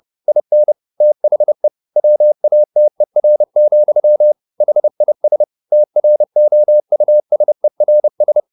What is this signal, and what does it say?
Morse code, heard as a keyed tone, about 30 wpm, 600 Hz. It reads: IN THE WATER, HIS TROUSERS